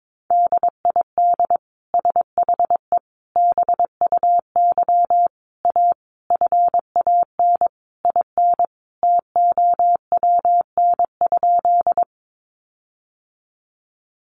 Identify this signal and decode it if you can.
Morse code, heard as a keyed tone, about 22 wpm, 700 Hz. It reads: DID SHE BUY A FAN IN TOWN?
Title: Sentences from Top 1300 Words 22wpm